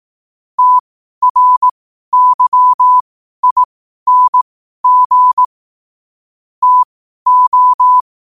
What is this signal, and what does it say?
Morse code, heard as a keyed tone, about 18 wpm, spaced as for 12 wpm, 1000 Hz. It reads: TRYING TO
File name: qotdfiles / 12wpm